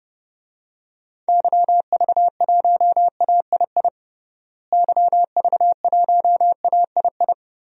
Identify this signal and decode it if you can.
Morse code, heard as a keyed tone, about 30 wpm, 700 Hz. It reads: YV1AII YV1AII
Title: DXCC-100 Call Signs - 6 Characters - Repeated 2x - 30wpm